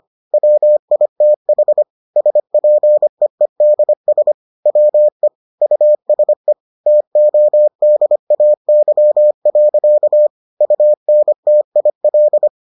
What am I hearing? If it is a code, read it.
Morse code, heard as a keyed tone, about 25 wpm, 600 Hz. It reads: WITH SPEEDS WE USE TODAY. UNTIL